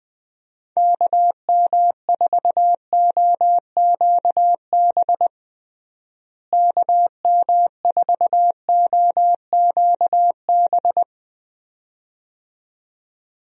Morse code, heard as a keyed tone, about 21 wpm, 700 Hz. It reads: KM4OQB KM4OQB
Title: USA Call Signs - 6 Characters - Repeated 2x - 20wpm